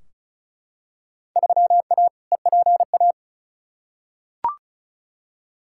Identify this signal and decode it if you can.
Morse code, heard as a keyed tone, about 36 wpm, 700 Hz. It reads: 3A EPA
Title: Field Day Exchanges - 35wpm